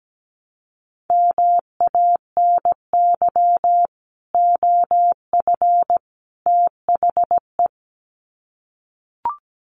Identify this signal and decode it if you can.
Morse code, heard as a keyed tone, about 17 wpm, 700 Hz. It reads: MANY OF THE
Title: Sets of 3 Words 17wpm